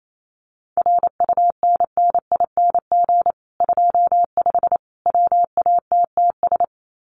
Morse code, heard as a keyed tone, about 28 wpm, 700 Hz. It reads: RUNNING 25 WATTS
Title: Common QSO Phrases 28wpm